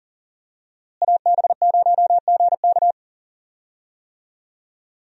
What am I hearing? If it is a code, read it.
Morse code, heard as a keyed tone, about 37 wpm, 700 Hz. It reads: AB0GK
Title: USA Call Signs - 5 Characters - Repeated 2x - 40wpm